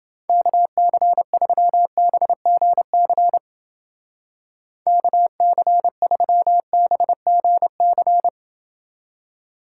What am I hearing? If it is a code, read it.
Morse code, heard as a keyed tone, about 28 wpm, 700 Hz. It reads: KC3BGC KC3BGC